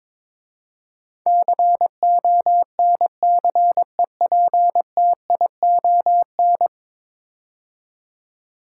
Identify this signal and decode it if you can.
Morse code, heard as a keyed tone, about 22 wpm, 700 Hz. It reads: CONCEPTION
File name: practice / 22wpm / 10 Letter Words - First 3 Letters Given - 22wpm